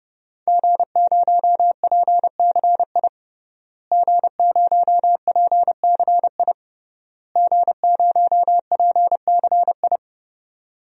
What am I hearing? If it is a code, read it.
Morse code, heard as a keyed tone, about 30 wpm, 700 Hz. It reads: G0PCI G0PCI G0PCI
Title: DXCC-100 Call Signs - 5 Characters - Repeated 3x - 30wpm